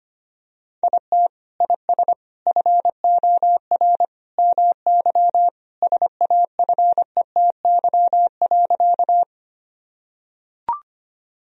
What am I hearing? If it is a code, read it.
Morse code, heard as a keyed tone, about 25 wpm, 700 Hz. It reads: IT IS FOR MY SAFETY.
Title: Sentences from Top 1000 Words - Review Part 4 - 25wpm